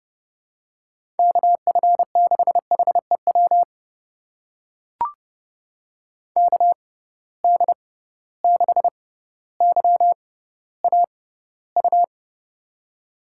Morse code, heard as a keyed tone, about 30 wpm, spaced as for 15 wpm, 700 Hz. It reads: KF6HEW KD6YAU
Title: USA Call Signs - 6 Characters - 6x Character Spacing - 30wpm